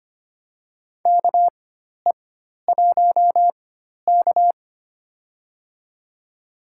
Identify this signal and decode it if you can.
Morse code, heard as a keyed tone, about 25 wpm, spaced as for 12 wpm, 700 Hz. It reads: KE1K
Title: USA Call Signs - 4 Characters - 4x Character Spacing - 25wpm